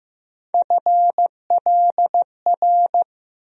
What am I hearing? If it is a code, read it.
Morse code, heard as a keyed tone, about 15 wpm, 700 Hz. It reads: FLR
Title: USA Call Sign Postfixes - 3 Letters 15wpm